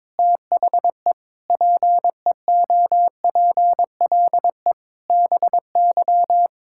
Morse code, heard as a keyed tone, about 22 wpm, 700 Hz. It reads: THE PEOPLE BY
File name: Sentences with 9-Word Phrases - 22wpm